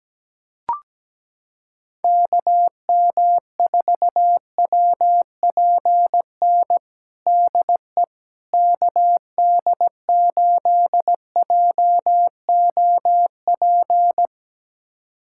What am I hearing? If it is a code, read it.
Morse code, heard as a keyed tone, about 17 wpm, 700 Hz. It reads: KM4WPN DE KD8JOP
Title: USA Callsign DE Callsign - 6 Characters - 17wpm